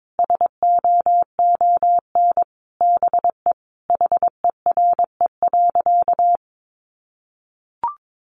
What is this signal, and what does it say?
Morse code, heard as a keyed tone, about 22 wpm, 700 Hz. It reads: SOON BE HERE.
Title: Sentences from Top 1200 Words 22wpm